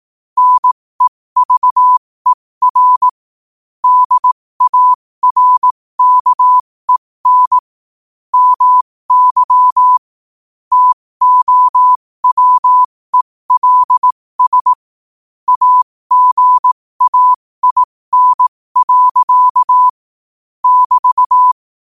Morse code, heard as a keyed tone, about 18 wpm, spaced as for 15 wpm, 1000 Hz. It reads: NEVER DARKEN MY TOWELS AGAIN. <BT>
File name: qotdfiles / 15wpm